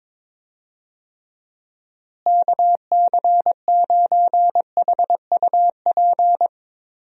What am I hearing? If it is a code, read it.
Morse code, heard as a keyed tone, about 22 wpm, 700 Hz. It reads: KC9HUP
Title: USA Call Signs - 6 Characters - Repeated 3x - 22wpm